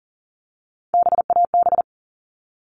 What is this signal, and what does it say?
Morse code, heard as a keyed tone, about 40 wpm, 700 Hz. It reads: BAB
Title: All Cut Numbers - 3-Digit 40wpm